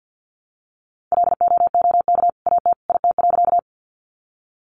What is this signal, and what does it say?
Morse code, heard as a keyed tone, about 50 wpm, 700 Hz. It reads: LOOK AT IT.
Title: Sentences from Top 100 Words 50wpm